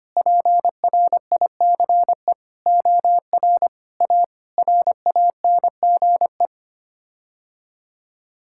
Morse code, heard as a keyed tone, about 25 wpm, 700 Hz. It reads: PRICE OR A RANGE
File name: Sets of 9 Words - 25wpm